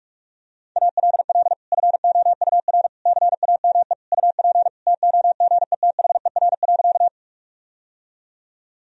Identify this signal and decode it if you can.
Morse code, heard as a keyed tone, about 47 wpm, 700 Hz. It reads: ALL FOUR CAME UP TOGETHER.
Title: Sentences from Top 1000 Words - Review Part 1 - 45wpm